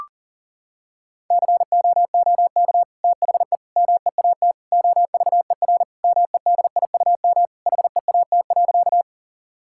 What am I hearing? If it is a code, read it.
Morse code, heard as a keyed tone, about 40 wpm, 700 Hz. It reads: COOK THE MEAT OVER MEDIUM HEAT.
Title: Sentences from Top 1000 Words - Review Part 6 - 40wpm